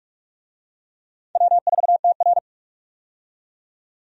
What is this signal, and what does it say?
Morse code, heard as a keyed tone, about 47 wpm, 700 Hz. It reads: W4TR
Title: USA Call Signs - 4 Characters - 45wpm